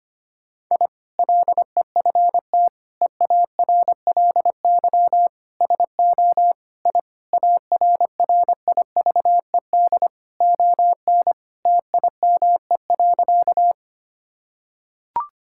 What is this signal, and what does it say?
Morse code, heard as a keyed tone, about 25 wpm, 700 Hz. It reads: I LEFT EARLY SO I ARRIVED ON TIME.
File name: practice / 25wpm / Sentences with 2-Word Phrases - 25wpm